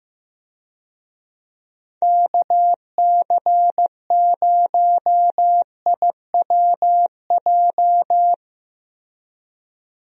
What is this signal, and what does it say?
Morse code, heard as a keyed tone, about 15 wpm, 700 Hz. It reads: KC0IWJ
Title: USA Call Signs - 6 Characters - Repeated 3x - 15wpm